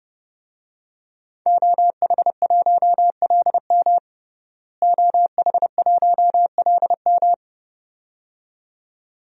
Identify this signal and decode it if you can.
Morse code, heard as a keyed tone, about 30 wpm, 700 Hz. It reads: OH1LM OH1LM
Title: DXCC-100 Call Signs - 5 Characters - Repeated 2x - 30wpm